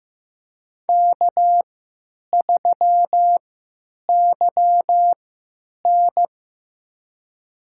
Morse code, heard as a keyed tone, about 15 wpm, spaced as for 9 wpm, 700 Hz. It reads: K3YN
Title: USA Call Signs - 4 Characters - 3x Character Spacing - 15wpm